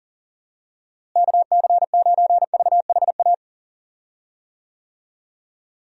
Morse code, heard as a keyed tone, about 40 wpm, 700 Hz. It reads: KC9VHA
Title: USA Call Signs - 6 Characters - Repeated 3x - 40wpm